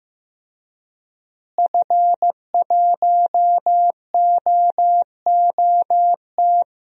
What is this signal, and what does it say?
Morse code, heard as a keyed tone, about 16 wpm, 700 Hz. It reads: F1OOT
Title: DXCC-100 Call Signs - 5 Characters - Repeated 2x - 15wpm